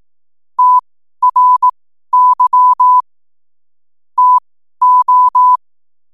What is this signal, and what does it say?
Morse code, heard as a keyed tone, about 18 wpm, spaced as for 12 wpm, 1000 Hz. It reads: TRY TO
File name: n7net / 12wpm